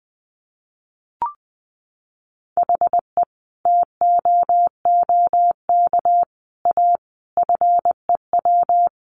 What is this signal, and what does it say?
Morse code, heard as a keyed tone, about 20 wpm, 700 Hz. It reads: HE TOOK A FEW
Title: A Sound Of Thunder 20wpm